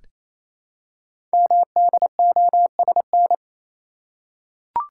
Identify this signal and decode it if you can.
Morse code, heard as a keyed tone, about 28 wpm, 700 Hz. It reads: MDOSN
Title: Code Groups - 5 Letter:Numbers - 28wpm